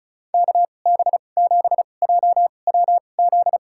Morse code, heard as a keyed tone, about 34 wpm, spaced as for 25 wpm, 700 Hz. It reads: KB7JWZ
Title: USA Call Signs - 2x Character Spacing - 35wpm